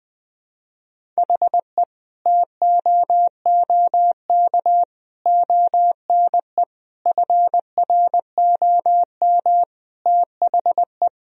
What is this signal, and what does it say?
Morse code, heard as a keyed tone, about 20 wpm, 700 Hz. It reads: HE TOOK ONE FROM THE
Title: Sentences from Top 1000 Words - Review Part 1 - 20wpm